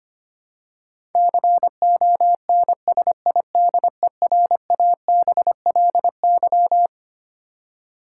Morse code, heard as a keyed tone, about 25 wpm, 700 Hz. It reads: CONSIDERABLY
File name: Twelve Letter Words - 25wpm